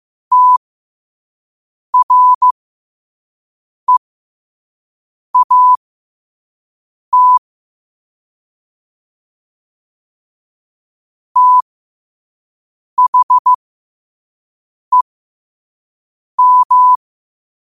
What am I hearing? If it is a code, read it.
Morse code, heard as a keyed tone, about 15 wpm, spaced as for 5 wpm, 1000 Hz. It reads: TREAT THEM
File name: qotdfiles / 5wpm